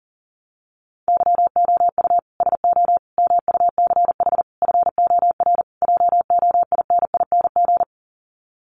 Morse code, heard as a keyed tone, about 40 wpm, 700 Hz. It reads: YOU SO MUCH FOR JOINING
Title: Sets of 5 Words Encore2 40wpm